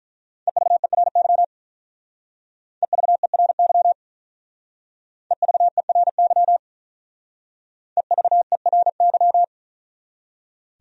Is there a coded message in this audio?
Yes: Morse code from 0.5 to 9.4 s, heard as a keyed tone, about 45 wpm, 700 Hz. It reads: EVERY EVERY EVERY EVERY